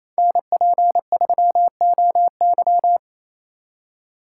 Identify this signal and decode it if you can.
Morse code, heard as a keyed tone, about 28 wpm, 700 Hz. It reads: NP3OY
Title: USA Call Signs - 5 Characters - 6x Character Spacing - 28wpm